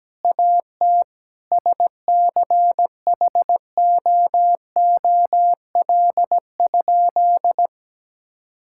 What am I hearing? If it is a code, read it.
Morse code, heard as a keyed tone, about 17 wpm, 700 Hz. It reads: AT SCHOOL?
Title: Sentences from Top 1000 Words - Review Part 6 - 17wpm